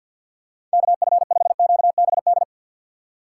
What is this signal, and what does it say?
Morse code, heard as a keyed tone, about 53 wpm, 700 Hz. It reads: KF5QBD
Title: USA Call Signs - 3x Character Spacing - 50wpm